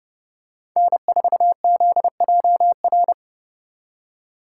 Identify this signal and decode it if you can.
Morse code, heard as a keyed tone, about 29 wpm, 700 Hz. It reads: N4ZJR